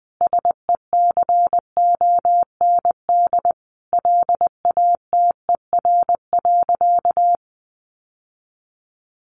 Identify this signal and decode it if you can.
Morse code, heard as a keyed tone, about 20 wpm, 700 Hz. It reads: SECOND LATER.
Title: Sentences from Top 900 Words 20wpm